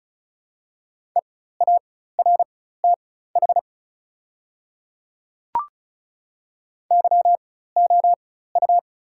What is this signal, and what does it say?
Morse code, heard as a keyed tone, about 35 wpm, spaced as for 16 wpm, 700 Hz. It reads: EARTH YOU
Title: 4x Character Spacing - Top 200 Words - 35wpm